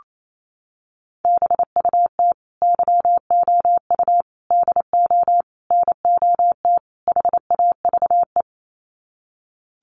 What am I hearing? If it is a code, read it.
Morse code, heard as a keyed tone, about 28 wpm, 700 Hz. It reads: BUT YOU DO NOT HAVE
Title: Sets of 5 Words Encore 28wpm